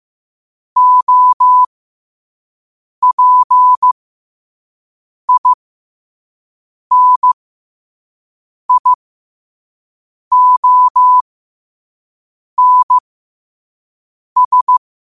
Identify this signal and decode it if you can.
Morse code, heard as a keyed tone, about 15 wpm, spaced as for 5 wpm, 1000 Hz. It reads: OPINIONS
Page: Quotes for Wed, 13 Aug 2025 in Morse Code at 5 words per minute.